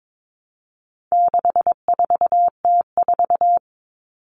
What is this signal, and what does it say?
Morse code, heard as a keyed tone, about 22 wpm, 700 Hz. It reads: 64T4
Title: All Cut Numbers - 4-Digit 22wpm